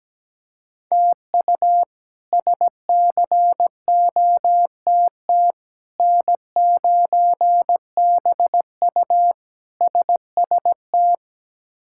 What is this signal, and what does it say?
Morse code, heard as a keyed tone, about 17 wpm, 700 Hz. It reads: TU SCOTT N9BU SST
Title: K1USN SST QSO Practice - 17wpm